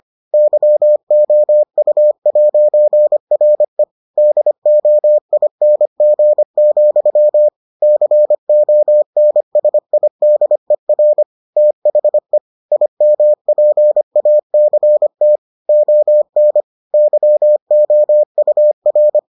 Morse code, heard as a keyed tone, about 25 wpm, 600 Hz. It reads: YOU'RE DOING, CONSIDER THE IMPACT ON YOUR